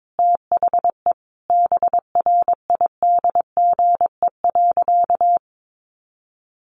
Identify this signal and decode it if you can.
Morse code, heard as a keyed tone, about 22 wpm, 700 Hz. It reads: THE BRIDGE.
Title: Sentences from Top 1200 Words 22wpm